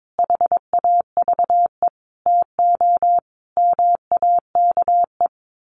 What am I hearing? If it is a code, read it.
Morse code, heard as a keyed tone, about 22 wpm, 700 Hz. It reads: HAVE TO MAKE
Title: Sets of 5 Words Encore2 22wpm